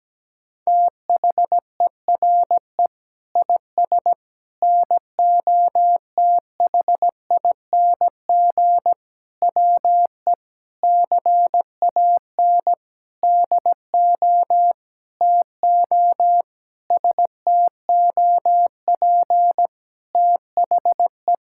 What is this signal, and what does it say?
Morse code, heard as a keyed tone, about 17 wpm, 700 Hz. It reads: THERE IS NOTHING WE CAN DO TO STOP THE
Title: Sentences from Top 1000 Words - Review Part 6 - 17wpm